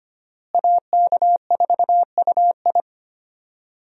Morse code, heard as a keyed tone, about 25 wpm, 700 Hz. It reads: AK4UI
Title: USA Call Signs - 5 Characters - Repeated 3x - 25wpm